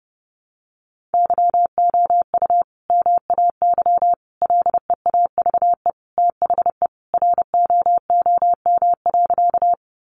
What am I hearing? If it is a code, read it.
Morse code, heard as a keyed tone, about 30 wpm, 700 Hz. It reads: YOU MAY LEAVE THE ROOM.
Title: Sentences from Top 400 Words 30wpm